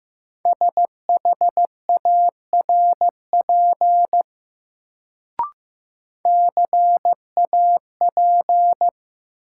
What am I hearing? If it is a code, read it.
Morse code, heard as a keyed tone, about 15 wpm, 700 Hz. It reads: SHARP CAP
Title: Sentences from Top 1600 Words - New Word Drill - 15wpm